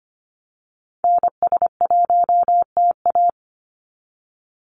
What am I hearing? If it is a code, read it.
Morse code, heard as a keyed tone, about 25 wpm, 700 Hz. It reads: NS1TA
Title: NA SOTA Activator Callsigns 25wpm